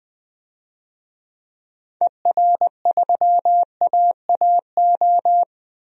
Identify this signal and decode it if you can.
Morse code, heard as a keyed tone, about 20 wpm, 700 Hz. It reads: ER3AAO
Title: DXCC-100 Call Signs - 6 Characters - 20wpm